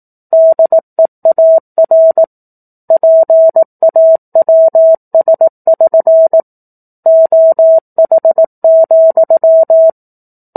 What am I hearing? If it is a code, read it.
Morse code, heard as a keyed tone, about 18 wpm, 650 Hz. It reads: DEAR PAWS<SN> OH,
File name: 04-Chapter-04 18 WPM